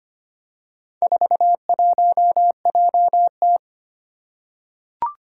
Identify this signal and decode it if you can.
Morse code, heard as a keyed tone, about 25 wpm, 700 Hz. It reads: 41JT
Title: Four Challenging Letter-Number Combinations 25wpm